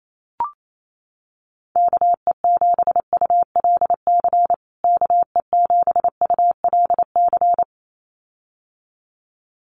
Morse code, heard as a keyed tone, about 28 wpm, 700 Hz. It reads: KE7ULC KE7ULC
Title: USA Call Signs Repeated 2x 28wpm